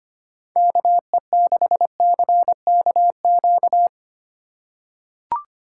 Morse code, heard as a keyed tone, about 25 wpm, 700 Hz. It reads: KE6CKQ
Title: USA Call Signs - 3x Character Spacing - 25wpm